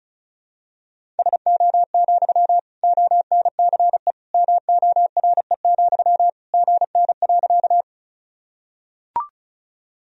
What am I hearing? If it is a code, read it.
Morse code, heard as a keyed tone, about 36 wpm, 700 Hz. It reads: SO, ONCE MORE, GN.
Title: Sentences from Top 300 Words and Abbreviations 35wpm